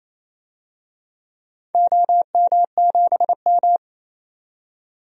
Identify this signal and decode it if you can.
Morse code, heard as a keyed tone, about 28 wpm, 700 Hz. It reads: OM7M
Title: DXCC-100 Call Signs - 4 Characters - 28wpm